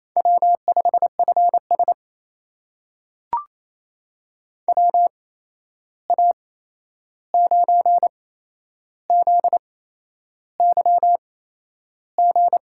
Morse code, heard as a keyed tone, about 27 wpm, spaced as for 10 wpm, 700 Hz. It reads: W5FS WA9ZYG
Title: USA Call Signs - 8x Character Spacing - 28wpm